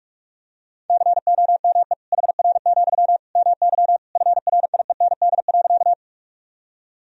Morse code, heard as a keyed tone, about 42 wpm, 700 Hz. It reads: COME HR, MY FRIEND.
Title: Sentences from Top 300 Words and Abbreviations 45wpm